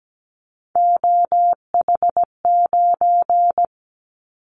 Morse code, heard as a keyed tone, about 17 wpm, 700 Hz. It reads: OH9
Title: Callsign Prefixes for Top 50 DXCC - 17wpm